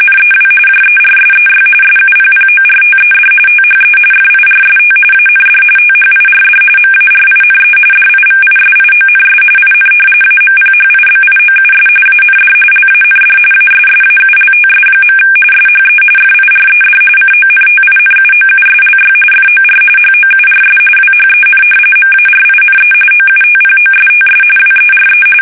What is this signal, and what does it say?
RTTY, about 45.45 baud. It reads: NEW JERSEY SUPPORTED THE FIRST AND THIRD MARINE DIVISIONS IN AMPHIBIOUS AND LAND ASSAULTS. SHE FIRED CONTINUOUSLY FOR HOURS WHEN MARINE OUTPOSTS WERE
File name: fsk850 / battleshipnj